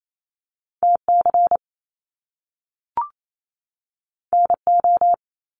Morse code, heard as a keyed tone, about 28 wpm, 700 Hz. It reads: TC NO
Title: Top Two Letter Combinations in Words 28wpm